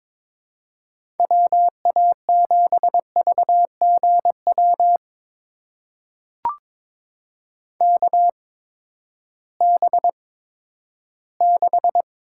Morse code, heard as a keyed tone, about 21 wpm, spaced as for 12 wpm, 700 Hz. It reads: WA7VGW KB6
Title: USA Call Signs - 8x Character Spacing - 22wpm